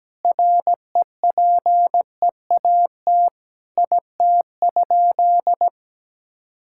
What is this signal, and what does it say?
Morse code, heard as a keyed tone, about 17 wpm, 700 Hz. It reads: REPEAT IT?
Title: Sentences from Top 1600 Words 17wpm